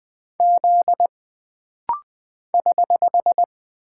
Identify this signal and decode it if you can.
Morse code, heard as a keyed tone, about 20 wpm, 700 Hz. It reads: Z <HH>
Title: Single Character (Letter:Number:Punctuation:Prosign) - Rapid-Fire - 20wpm